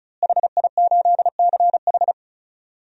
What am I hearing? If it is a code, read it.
Morse code, heard as a keyed tone, about 36 wpm, 700 Hz. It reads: HI8CH
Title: DXCC-100 Call Signs - 5 Characters - Repeated 3x - 35wpm